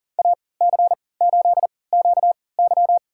Morse code, heard as a keyed tone, about 40 wpm, spaced as for 23 wpm, 700 Hz. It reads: AC8QY
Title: USA Call Signs - 3x Character Spacing - 40wpm